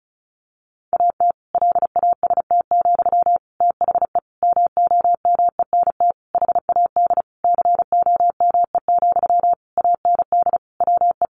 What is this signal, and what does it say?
Morse code, heard as a keyed tone, about 34 wpm, 700 Hz. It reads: AT LAST, THE MOMENT HAD COME, AND WE
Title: Sentences from Top 1500 Words 35wpm